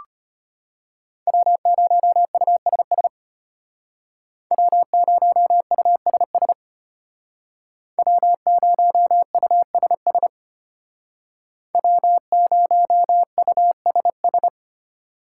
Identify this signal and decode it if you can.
Morse code, heard as a keyed tone, about 30 wpm, 700 Hz. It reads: W0USS W0USS W0USS W0USS
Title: Speed Racing - USA 1x3 Call Signs 25wpm